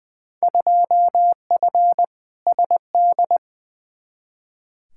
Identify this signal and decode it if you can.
Morse code, heard as a keyed tone, about 20 wpm, 700 Hz. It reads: 2F SD
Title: Field Day Exchanges Encore - 20wpm